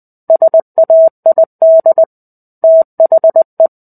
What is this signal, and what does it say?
Morse code, heard as a keyed tone, about 20 wpm, 650 Hz. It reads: SAID THE
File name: 05-Chapter-05 20 WPM